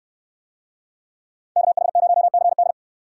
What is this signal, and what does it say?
Morse code, heard as a keyed tone, about 69 wpm, 700 Hz. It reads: KH0CD